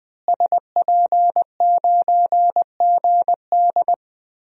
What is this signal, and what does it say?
Morse code, heard as a keyed tone, about 20 wpm, 700 Hz. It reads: SP9GD
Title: DXCC-100 Call Signs - 5 Characters - Repeated 3x - 20wpm